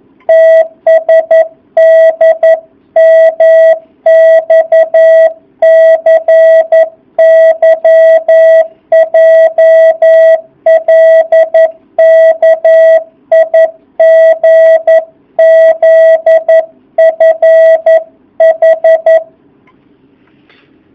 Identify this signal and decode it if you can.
Morse code, heard as a keyed tone, about 11 wpm, 650 Hz. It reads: TSDMXCYJLKIGZFH
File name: jh8cbh / 2019cw